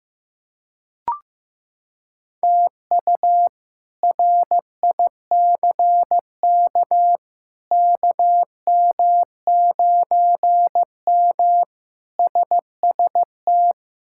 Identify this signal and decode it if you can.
Morse code, heard as a keyed tone, about 15 wpm, 700 Hz. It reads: TU RICK KM9M SST
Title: K1USN SST QSO Practice - 15wpm